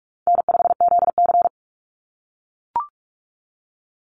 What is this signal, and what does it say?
Morse code, heard as a keyed tone, about 44 wpm, 700 Hz. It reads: N5ZC
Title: NA SOTA Chaser Callsigns 45wpm